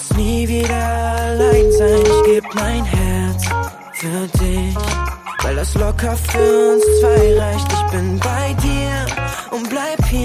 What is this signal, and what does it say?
Ringback tone (Europe, 425 Hz), with noise.